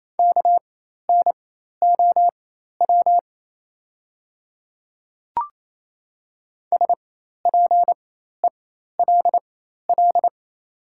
Morse code, heard as a keyed tone, about 28 wpm, spaced as for 13 wpm, 700 Hz. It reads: KNOW SPELL
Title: 4x Character Spacing - Top 300 Words - 28wpm